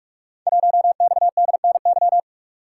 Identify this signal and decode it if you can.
Morse code, heard as a keyed tone, about 44 wpm, 700 Hz. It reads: 1XDNY
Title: Code Groups - 5 Letter:Numbers - 45wpm